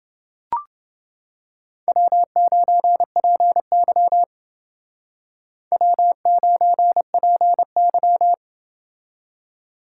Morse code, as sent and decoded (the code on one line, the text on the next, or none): .-- ----. .--. -.-- / .-- ----. .--. -.--
W9PY W9PY